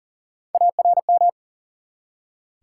Morse code, heard as a keyed tone, about 39 wpm, 700 Hz. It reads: ARM
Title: Three Letter Words - 40wpm